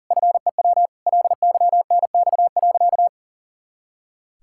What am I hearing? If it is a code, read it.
Morse code, heard as a keyed tone, about 40 wpm, 700 Hz. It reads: FEW LYNX.
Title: Pangram Challenge v2 - 40wpm